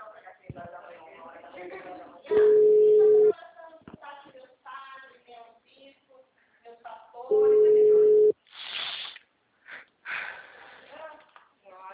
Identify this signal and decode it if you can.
Ringback tone (Europe, 425 Hz), with speech.